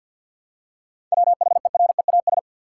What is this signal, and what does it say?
Morse code, heard as a keyed tone, about 48 wpm, 700 Hz. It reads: WHEREAS
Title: Seven Letter Words - 50wpm